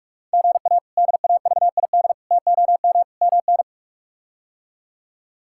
Morse code, heard as a keyed tone, about 46 wpm, 700 Hz. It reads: GA DAVID TOM MN
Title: K1USN SST QSO Practice - 45wpm